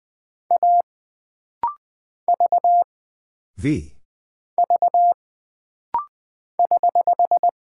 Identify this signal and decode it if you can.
Morse code, heard as a keyed tone, about 20 wpm, 700 Hz. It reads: A V V <HH>
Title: Single Character (Letter:Number:Punctuation:Prosign) - Rapid-Fire - 20wpm